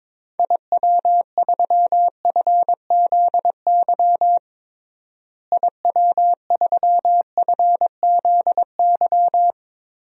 Morse code, heard as a keyed tone, about 21 wpm, 700 Hz. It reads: IW3FZY IW3FZY
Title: DXCC-100 Call Signs - 6 Characters - Repeated 3x - 22wpm